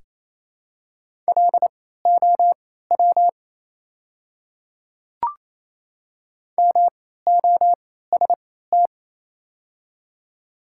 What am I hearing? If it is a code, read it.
Morse code, heard as a keyed tone, about 28 wpm, spaced as for 16 wpm, 690 Hz. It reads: LOW MOST
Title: 3x Character Spacing - Top 200 Words - 28wpm